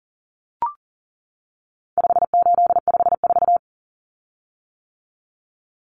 Morse code, heard as a keyed tone, about 40 wpm, 700 Hz. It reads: 5854
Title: Common Cut Numbers - 4-Digit 40wpm